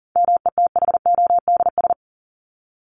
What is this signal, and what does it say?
Morse code, heard as a keyed tone, about 40 wpm, 700 Hz. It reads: METHODS
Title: ES:S Suffix - Top 100 Words with Base Words 40wpm